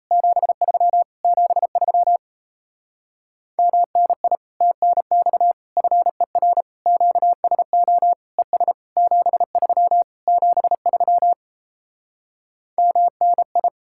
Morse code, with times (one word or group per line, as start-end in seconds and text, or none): 0.1-1.0 73
1.2-2.2 73
3.6-4.4 MNI
4.6-5.5 TNX
5.8-6.6 FER
6.9-8.1 QSO
8.4-8.7 ES
9.0-10.0 73
10.3-11.3 73
12.8-13.7 MNI